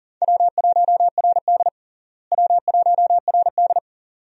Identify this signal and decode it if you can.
Morse code, heard as a keyed tone, about 37 wpm, 700 Hz. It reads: W1RD W1RD
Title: USA Call Signs - 4 Characters - Repeated 2x - 40wpm